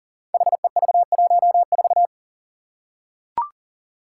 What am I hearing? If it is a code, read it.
Morse code, heard as a keyed tone, about 40 wpm, 700 Hz. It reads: HEV14
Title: Five Challenging Letter-Number Combinations 40wpm